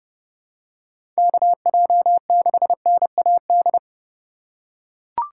Morse code, heard as a keyed tone, about 32 wpm, 700 Hz. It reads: KJ6NAD
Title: USA Call Signs - 6 Characters - 3x Character Spacing - 30wpm